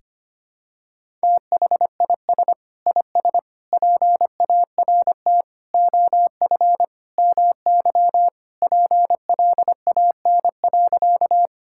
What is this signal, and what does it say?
Morse code, heard as a keyed tone, about 25 wpm, 700 Hz. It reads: THIS IS PART OF MY PLAN.
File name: practice / 25wpm / Sentences with 2-Word Phrases - 25wpm